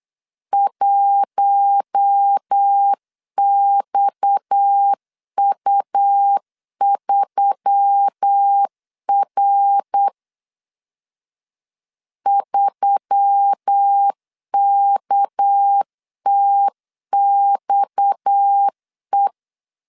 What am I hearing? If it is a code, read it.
Morse code, heard as a keyed tone, about 8 wpm, 800 Hz. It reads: 1XU3R 3KTXE
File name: LEZIONI / 6 PKX0 / 6 PKX 8wpm 3041